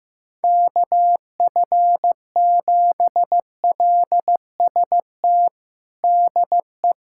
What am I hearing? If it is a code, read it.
Morse code, heard as a keyed tone, about 15 wpm, 700 Hz. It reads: KF7LST DE
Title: USA Callsign DE Callsign 15wpm